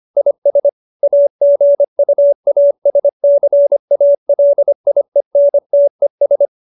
Morse code, heard as a keyed tone, about 25 wpm, 570 Hz. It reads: IS AGUASCALIENTES
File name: QSO304 25